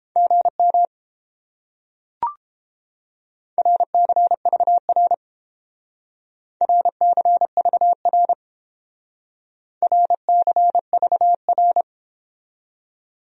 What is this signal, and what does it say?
Morse code, heard as a keyed tone, about 30 wpm, 700 Hz. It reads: GM RCVR RCVR RCVR